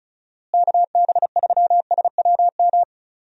Morse code, heard as a keyed tone, about 34 wpm, 700 Hz. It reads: KB3SWM
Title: USA Call Signs - 6 Characters - Repeated 3x - 35wpm